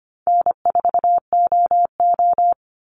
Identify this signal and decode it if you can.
Morse code, heard as a keyed tone, about 25 wpm, 700 Hz. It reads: N4OO
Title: Call Signs Repeated 3x - 25wpm